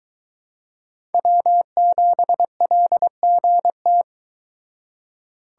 Morse code, heard as a keyed tone, about 23 wpm, 700 Hz. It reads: W7LGT